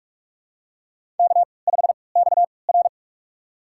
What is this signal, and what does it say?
Morse code, heard as a keyed tone, about 46 wpm, spaced as for 25 wpm, 700 Hz. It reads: K5XR
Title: USA Call Signs - 4 Characters - 3x Character Spacing - 45wpm